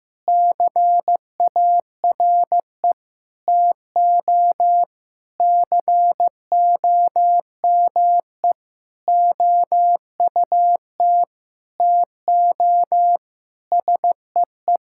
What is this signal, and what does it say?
Morse code, heard as a keyed tone, about 16 wpm, 700 Hz. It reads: CARE TO COME OUT TO SEE
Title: Sentences from Top 1000 Words - Review Part 6 - 15wpm